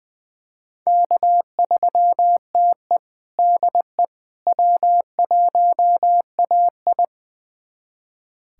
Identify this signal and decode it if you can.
Morse code, heard as a keyed tone, about 21 wpm, 700 Hz. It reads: K3TE DE W1AI
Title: USA Callsign DE Callsign - 4 Characters - 20wpm